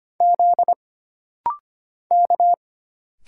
Morse code, heard as a keyed tone, about 25 wpm, 700 Hz. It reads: Z K
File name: Single Character (Letter:Number:Punctuation:Prosign) - Rapid-Fire - 25wpm